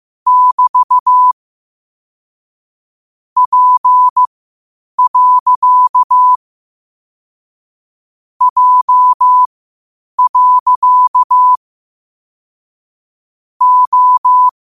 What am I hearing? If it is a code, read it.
Morse code, heard as a keyed tone, about 15 wpm, spaced as for 8 wpm, 1000 Hz. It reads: <BT> P. J. O
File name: qotdfiles / 8wpm